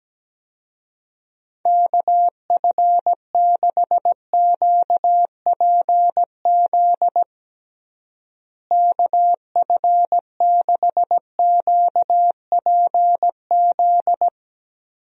Morse code, heard as a keyed tone, about 17 wpm, 700 Hz. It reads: KF6QPZ KF6QPZ
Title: USA Call Signs - 6 Characters - Repeated 3x - 17wpm